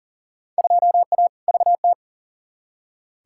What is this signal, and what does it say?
Morse code, heard as a keyed tone, about 40 wpm, 700 Hz. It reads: 2A VT
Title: Field Day Exchanges Encore - 40wpm